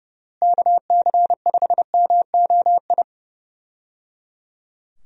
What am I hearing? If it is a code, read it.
Morse code, heard as a keyed tone, about 29 wpm, 700 Hz. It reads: KC5MOI
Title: USA Call Signs - 6 Characters - Repeated 3x - 30wpm